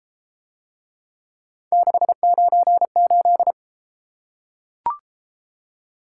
Morse code, heard as a keyed tone, about 28 wpm, 700 Hz. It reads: 698